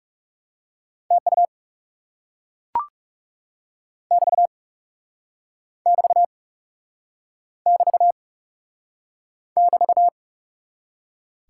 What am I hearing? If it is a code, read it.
Morse code, heard as a keyed tone, about 40 wpm, 700 Hz. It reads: TU <BT> <BT> <BT> <BT>